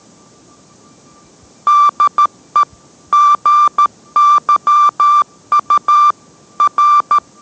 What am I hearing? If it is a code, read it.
Morse code, heard as a keyed tone, about 13 wpm, 1180 Hz. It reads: DE GYU R